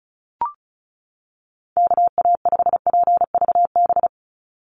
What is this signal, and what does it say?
Morse code, heard as a keyed tone, about 34 wpm, 700 Hz. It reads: KA5PVB
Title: NA SOTA Chaser Callsigns 35wpm